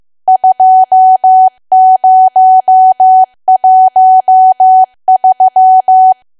Morse code, heard as a keyed tone, about 15 wpm, 750 Hz. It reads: 2013
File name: cwa / W1AW-Level2 / 15WPM